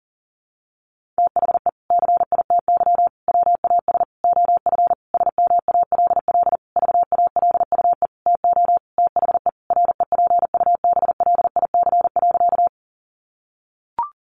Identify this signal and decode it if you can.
Morse code, heard as a keyed tone, about 37 wpm, 700 Hz. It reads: THE CITY WAS OF SMALL VALUE TO THE REPUBLIC.
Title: Sentences from Top 1500 Words 40wpm